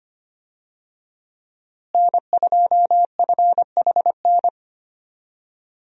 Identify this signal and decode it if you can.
Morse code, heard as a keyed tone, about 25 wpm, 700 Hz. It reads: N2FHN
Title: USA Call Signs - 5 Characters - Repeated 2x - 25wpm